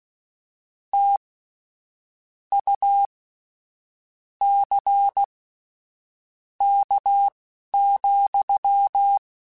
Morse code, heard as a keyed tone, about 16 wpm, spaced as for 6 wpm, 800 Hz. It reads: TUCK,
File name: Misc / Misc-mp3s / WordPairs sp @10